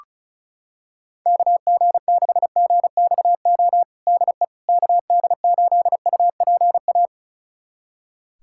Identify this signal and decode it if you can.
Morse code, heard as a keyed tone, about 36 wpm, 700 Hz. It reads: KG6GXO DE KD8UPA
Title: USA Callsign DE Callsign - 6 Characters - 35wpm